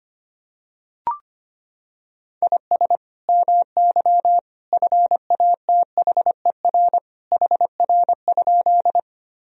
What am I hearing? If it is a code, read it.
Morse code, heard as a keyed tone, about 25 wpm, 700 Hz. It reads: IS MY FATHER HR?
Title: Sentences from Top 300 Words and Abbreviations 25wpm